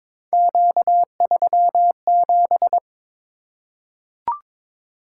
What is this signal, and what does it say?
Morse code, heard as a keyed tone, about 22 wpm, 700 Hz. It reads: Q37
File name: practice / 22wpm / Three Letter-Number Combinations - 22wpm